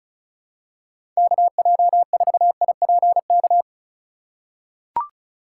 Morse code, heard as a keyed tone, about 36 wpm, 700 Hz. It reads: KJ4IPK
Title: USA Call Signs - 6 Characters - 8x Character Spacing - 35wpm